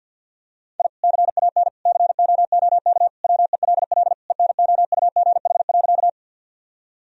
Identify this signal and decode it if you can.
Morse code, heard as a keyed tone, about 50 wpm, 700 Hz. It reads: I CAN COOK WELL ENOUGH.
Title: Sentences from Top 1000 Words - Review Part 4 - 50wpm